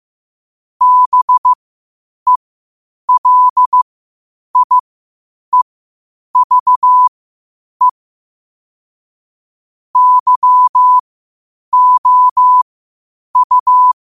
Morse code, heard as a keyed tone, about 15 wpm, spaced as for 8 wpm, 1000 Hz. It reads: BELIEVE YOU
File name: qotdfiles / 8wpm